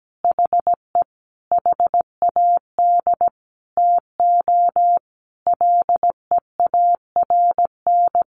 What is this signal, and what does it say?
Morse code, heard as a keyed tone, about 17 wpm, 700 Hz. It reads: HE HAD TO LEARN
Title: Sentences from Top 1200 Words 17wpm